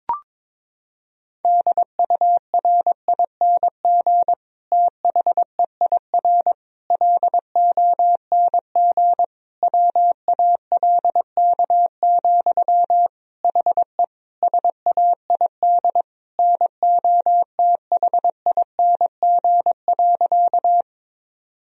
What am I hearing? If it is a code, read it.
Morse code, heard as a keyed tone, about 22 wpm, 700 Hz. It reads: DURING THEIR LONG WALK, HE SAID NOTHING.
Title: Sentences from Top 1000 Words - Review Part 2 - 22wpm